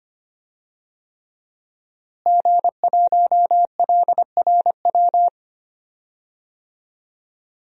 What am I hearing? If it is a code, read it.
Morse code, heard as a keyed tone, about 25 wpm, 700 Hz. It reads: G1LRW
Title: DXCC-100 Call Signs - 5 Characters - Repeated 3x - 25wpm